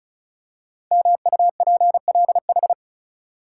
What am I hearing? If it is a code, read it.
Morse code, heard as a keyed tone, about 36 wpm, 700 Hz. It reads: MUPLH
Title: Code Groups - 5 Letters - 35wpm